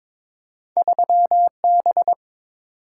Morse code, heard as a keyed tone, about 30 wpm, spaced as for 26 wpm, 700 Hz. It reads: 3B
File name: Speed-Racing - Two Letter-Number Combinations - 22wpm